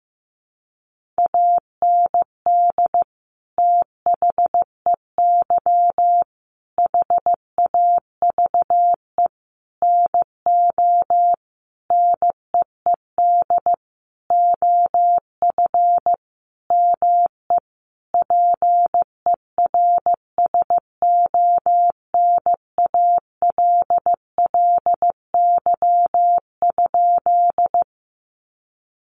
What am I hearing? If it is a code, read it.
Morse code, heard as a keyed tone, about 15 wpm, 700 Hz. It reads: AND THEY HAVE NO NEED OF ME PERSONALLY?
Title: Sentences from Top 1400 Words 15wpm